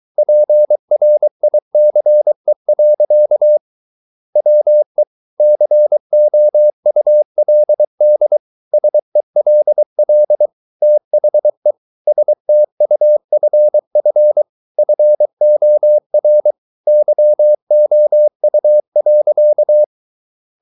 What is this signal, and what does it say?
Morse code, heard as a keyed tone, about 23 wpm, 600 Hz. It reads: PRICE. WE COULD SELL THE STUFF FOR YOU.